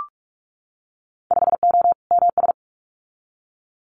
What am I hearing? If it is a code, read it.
Morse code, heard as a keyed tone, about 44 wpm, 700 Hz. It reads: 5O MS
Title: Winter Field Day Exchanges Encore 45wpm